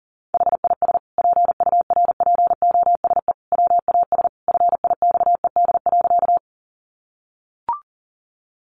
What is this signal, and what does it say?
Morse code, heard as a keyed tone, about 40 wpm, 700 Hz. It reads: HIS PURPOSE WAS FIXED.
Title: Sentences from Top 1100 Words 40wpm